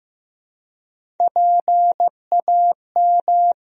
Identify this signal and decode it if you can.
Morse code, heard as a keyed tone, about 15 wpm, 700 Hz. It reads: PAM
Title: USA Common Names - Max 3 Letters - 15wpm